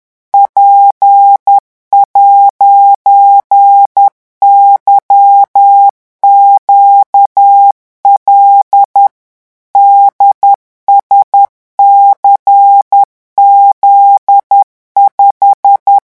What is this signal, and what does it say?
Morse code, heard as a keyed tone, about 10 wpm, 800 Hz. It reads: P'YQL DSCZ5
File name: morse mp3 / revision13-vitesse 10 mots